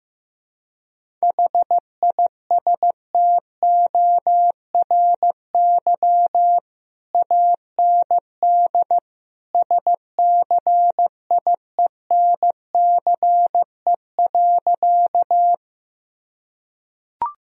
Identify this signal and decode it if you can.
Morse code, heard as a keyed tone, about 15 wpm, 700 Hz. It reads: HISTORY AND SCIENCE.
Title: Sentences from Top 1000 Words - Review Part 2 - 15wpm